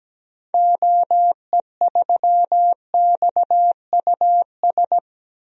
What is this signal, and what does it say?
Morse code, heard as a keyed tone, about 17 wpm, 700 Hz. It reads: OE3XUS
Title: DXCC-100 Call Signs - 6 Characters - 17wpm